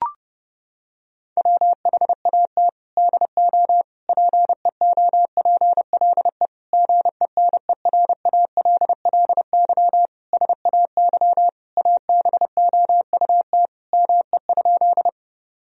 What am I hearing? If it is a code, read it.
Morse code, heard as a keyed tone, about 30 wpm, 700 Hz. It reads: WHAT DO PEOPLE GENERALLY SAY ABOUT ME?
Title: Sentences from Top 1000 Words - Review Part 4 - 30wpm